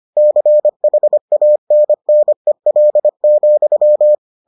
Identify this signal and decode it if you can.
Morse code, heard as a keyed tone, about 25 wpm, 600 Hz. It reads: CHANNEL,